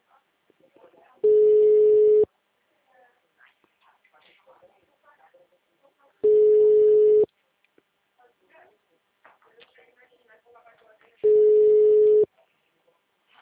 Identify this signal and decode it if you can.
Ringback tone (Europe, 425 Hz), with speech.